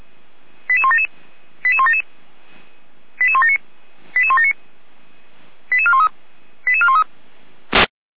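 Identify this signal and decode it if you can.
Five-tone selective call (ZVEI), heard as quick runs of five tones, tones of about 80 ms. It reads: 88177 88177 88179 88179 88413 88413
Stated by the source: ZVEI five-tone selective call received on SDR.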